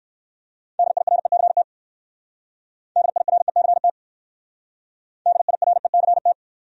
Morse code, heard as a keyed tone, about 56 wpm, 700 Hz. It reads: DIRECT DIRECT DIRECT
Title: Speed-Racing Top 500 Words 45wpm